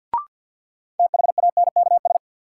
Morse code, heard as a keyed tone, about 53 wpm, 700 Hz. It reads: THANKS
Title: Sets of 5 Words - New Word Drill - 50wpm